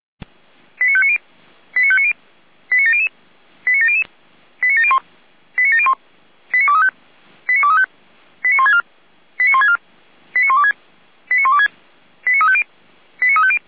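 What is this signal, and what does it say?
Five-tone selective call (ZVEI), heard as quick runs of five tones, tones of about 70 ms. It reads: 89590 89590 89700 89700 89701 89701 89235 89235 89164 89164 89126 89126 89350 89350
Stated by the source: ZVEI five-tone selective call 89092 received on an SDR.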